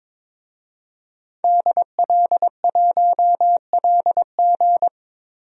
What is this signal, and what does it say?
Morse code, heard as a keyed tone, about 22 wpm, 700 Hz. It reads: DL1LG
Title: DXCC-100 Call Signs - 5 Characters - 22wpm